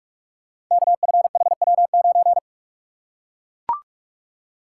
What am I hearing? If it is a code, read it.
Morse code, heard as a keyed tone, about 44 wpm, 690 Hz. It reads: KFHW9
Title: Code Groups - 5 Letter:Numbers - 45wpm